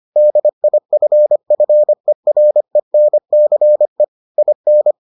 Morse code, heard as a keyed tone, about 25 wpm, 600 Hz. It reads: DIFFERENCE IN